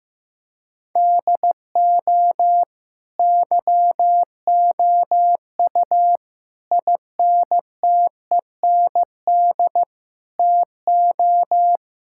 Morse code, heard as a keyed tone, about 15 wpm, 700 Hz. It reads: DO YOU INTEND TO
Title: Sentences from Top 1600 Words 15wpm